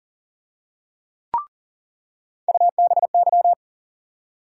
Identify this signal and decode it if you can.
Morse code, heard as a keyed tone, about 40 wpm, 700 Hz. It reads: UBY
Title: USA Call Sign Postfixes - 3 Letters 40wpm